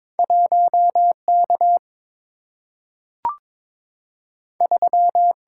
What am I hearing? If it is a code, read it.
Morse code, heard as a keyed tone, about 22 wpm, 700 Hz. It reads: 1K 3
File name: practice / 22wpm / Two Letter-Number Combinations - 22wpm